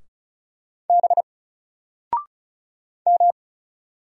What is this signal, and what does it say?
Morse code, heard as a keyed tone, about 35 wpm, 700 Hz. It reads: B M
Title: Single Character (Letter:Number:Punctuation:Prosign) 35wpm